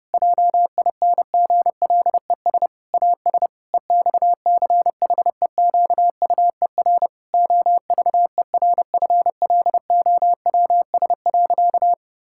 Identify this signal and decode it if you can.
Morse code, heard as a keyed tone, about 28 wpm, 700 Hz. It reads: JINGLES AS EXCHEQUER OVERFLOWS.
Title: Pangram Challenge v2 - 30wpm